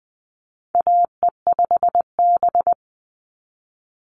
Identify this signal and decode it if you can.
Morse code, heard as a keyed tone, about 20 wpm, 700 Hz. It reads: AE5B
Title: NA SOTA Chaser Callsigns 20wpm